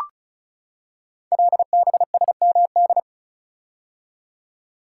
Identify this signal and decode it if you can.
Morse code, heard as a keyed tone, about 36 wpm, 700 Hz. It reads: LBSMD
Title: Code Groups - 5 Letters - 35wpm